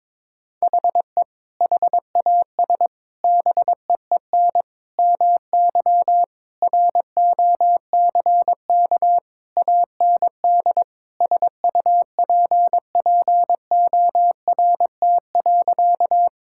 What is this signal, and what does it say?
Morse code, heard as a keyed tone, about 21 wpm, 700 Hz. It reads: HE HAS BEEN MY ROCK AND SUPPORT.
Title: Sentences from Top 1000 Words - Review Part 5 - 22wpm